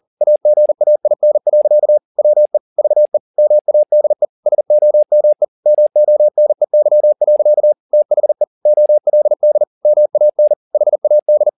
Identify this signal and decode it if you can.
Morse code, heard as a keyed tone, about 40 wpm, 600 Hz. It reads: AGAIN. WE VE MADE SOME MONEY. THE OLD MAN HAD